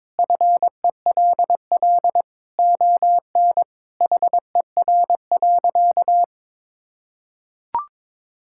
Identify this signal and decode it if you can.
Morse code, heard as a keyed tone, about 22 wpm, 700 Hz. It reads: FELL ON HER.
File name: practice / 22wpm / Sentences with 2-Word Phrases - 22wpm